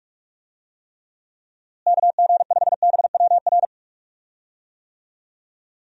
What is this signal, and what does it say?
Morse code, heard as a keyed tone, about 44 wpm, 700 Hz. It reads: KG5BWR
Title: USA Call Signs - 6 Characters - Repeated 3x - 45wpm